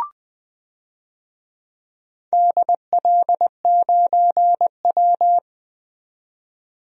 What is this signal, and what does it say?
Morse code, heard as a keyed tone, about 20 wpm, 700 Hz. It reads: DL9W
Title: DXCC-100 Call Signs - 4 Characters - 20wpm